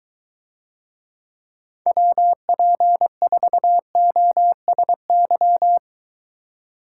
Morse code, heard as a keyed tone, about 22 wpm, 700 Hz. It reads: WP4OSY